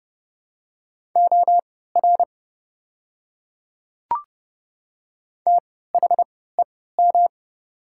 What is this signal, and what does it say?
Morse code, heard as a keyed tone, about 30 wpm, spaced as for 17 wpm, 700 Hz. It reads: OR THEM
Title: 3x Character Spacing - Top 200 Words - 30wpm